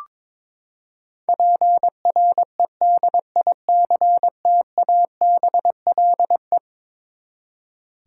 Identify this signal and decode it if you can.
Morse code, heard as a keyed tone, about 22 wpm, 700 Hz. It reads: PREDICTABLE
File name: practice / 22wpm / Eleven Letter Words - 22wpm